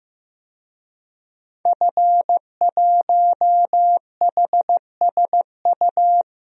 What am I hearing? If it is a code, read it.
Morse code, heard as a keyed tone, about 15 wpm, 700 Hz. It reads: F1HSU
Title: DXCC-100 Call Signs - 5 Characters - Repeated 2x - 15wpm